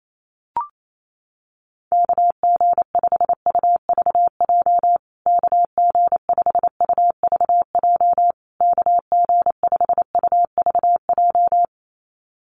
Morse code, heard as a keyed tone, about 28 wpm, 700 Hz. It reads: KG5UVJ KG5UVJ KG5UVJ
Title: USA Call Signs Repeated 3x 28wpm